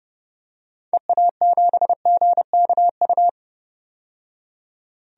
Morse code, heard as a keyed tone, about 31 wpm, 690 Hz. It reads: EA7GKU
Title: DXCC-100 Call Signs - 6 Characters - 30wpm